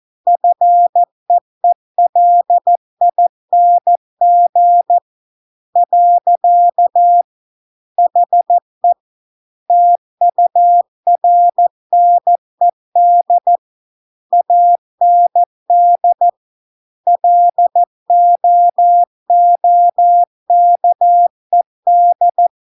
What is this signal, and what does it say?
Morse code, heard as a keyed tone, about 14 wpm, 700 Hz. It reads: FEELING . HE TURNED AND LOOKED